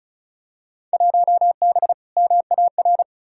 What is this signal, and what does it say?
Morse code, heard as a keyed tone, about 34 wpm, 700 Hz. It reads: 1B MAR
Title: Field Day Exchanges Encore - 35wpm